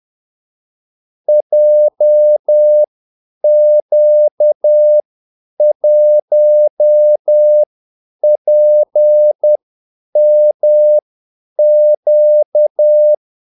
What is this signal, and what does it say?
Morse code, heard as a keyed tone, about 10 wpm, spaced as for 8 wpm, 600 Hz. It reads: JQ1PMQ
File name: Callsigns-10-8wpm